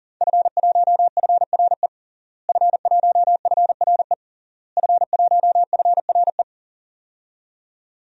Morse code, heard as a keyed tone, about 40 wpm, 700 Hz. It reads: F1FRE F1FRE F1FRE
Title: DXCC-100 Call Signs - 5 Characters - Repeated 3x - 40wpm